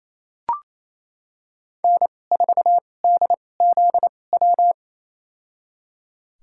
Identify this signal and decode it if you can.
Morse code, heard as a keyed tone, about 27 wpm, spaced as for 20 wpm, 700 Hz. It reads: N4DZW
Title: USA Call Signs - 2x Character Spacing - 28wpm